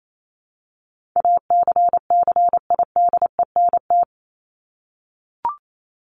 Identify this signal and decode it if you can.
Morse code, heard as a keyed tone, about 28 wpm, 700 Hz. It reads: ACCIDENT
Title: Top 1751-2000 Words 28wpm